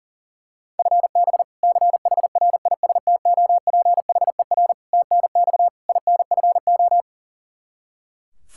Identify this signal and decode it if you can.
Morse code, heard as a keyed tone, about 40 wpm, 700 Hz. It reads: FB CHRISTOPHER TNX INFO
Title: K1USN SST QSO Practice 40wpm